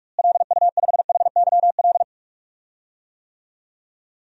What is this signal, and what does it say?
Morse code, heard as a keyed tone, about 45 wpm, 700 Hz. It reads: LU5HYL
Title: DXCC-100 Call Signs - 6 Characters - Repeated 2x - 45wpm